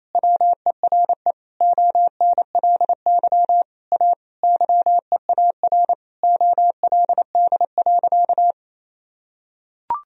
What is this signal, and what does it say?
Morse code, heard as a keyed tone, about 28 wpm, 700 Hz. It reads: WERE ONLY A YEAR OLD.
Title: Sentences from Top 1000 Words - Review Part 2 - 28wpm